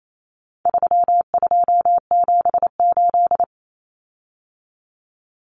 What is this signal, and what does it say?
Morse code, heard as a keyed tone, about 28 wpm, 700 Hz. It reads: 3278
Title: Common Cut Numbers - 4-Digit 28wpm